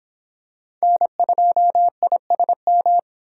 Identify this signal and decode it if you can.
Morse code, heard as a keyed tone, about 25 wpm, 700 Hz. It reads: N2ISM